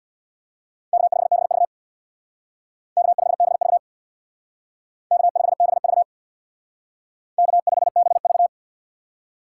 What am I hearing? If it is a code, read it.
Morse code, heard as a keyed tone, about 64 wpm, 700 Hz. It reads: K5BV K5BV K5BV K5BV